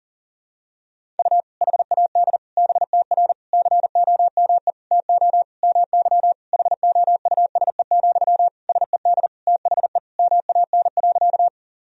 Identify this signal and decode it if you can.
Morse code, heard as a keyed tone, about 40 wpm, 700 Hz. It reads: U HAD BTR COME TO MY HOUSE, SED THE MAN.
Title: Sentences from Top 400 Words and Abbreviations 40wpm